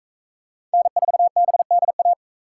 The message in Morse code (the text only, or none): N4BDA